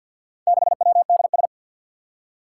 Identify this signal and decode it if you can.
Morse code, heard as a keyed tone, about 53 wpm, 700 Hz. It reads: 6WDS
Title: Four Challenging Letter-Number Combinations 50wpm